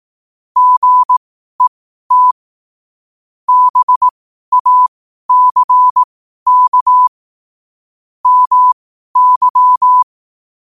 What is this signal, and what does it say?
Morse code, heard as a keyed tone, about 18 wpm, spaced as for 12 wpm, 1000 Hz. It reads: GET BACK MY
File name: qotdfiles / 12wpm